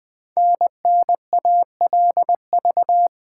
Morse code, heard as a keyed tone, about 20 wpm, 700 Hz. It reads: NNALV
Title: Code Groups - 5 Letter:Numbers - 20wpm